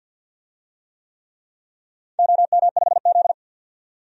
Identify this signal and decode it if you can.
Morse code, heard as a keyed tone, about 52 wpm, 700 Hz. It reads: OM5Z
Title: DXCC-100 Call Signs - 4 Characters - 50wpm